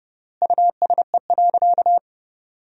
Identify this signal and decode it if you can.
Morse code, heard as a keyed tone, about 30 wpm, 700 Hz. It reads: USE.
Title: 3x Word-Spacing - Sentences from Top 100 Words - 30wpm